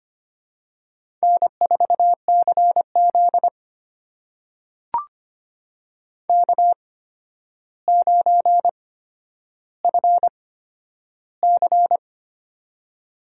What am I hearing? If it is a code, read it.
Morse code, heard as a keyed tone, about 25 wpm, spaced as for 11 wpm, 700 Hz. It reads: N4CZ K9FC
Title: USA Call Signs - 4 Characters - 8x Character Spacing - 25wpm